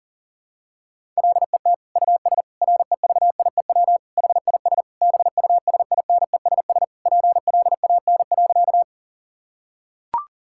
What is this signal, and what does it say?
Morse code, heard as a keyed tone, about 40 wpm, 700 Hz. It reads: LET US REVIEW HIS BUSINESS PLAN.
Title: Sentences from Top 1000 Words - Review Part 6 - 40wpm